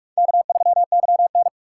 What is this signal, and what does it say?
Morse code, heard as a keyed tone, about 42 wpm, 700 Hz. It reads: K3YN
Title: USA Call Signs - 3x Character Spacing - 45wpm